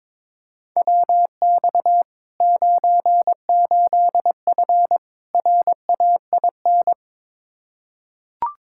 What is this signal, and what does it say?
Morse code, heard as a keyed tone, about 22 wpm, 700 Hz. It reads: WX 98F RAIN
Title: Practice QSO - Exchange 2 - Info - 22wpm